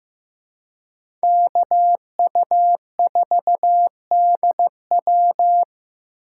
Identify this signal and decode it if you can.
Morse code, heard as a keyed tone, about 18 wpm, 700 Hz. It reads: KU4DW
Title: Speed Racing - USA 2x2 Call Signs 15wpm